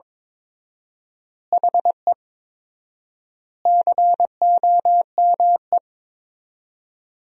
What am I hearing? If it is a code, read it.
Morse code, heard as a keyed tone, about 22 wpm, 700 Hz. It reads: HE COME
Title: 4x Word-Spacing - Sentences from Top 200 Words - 22wpm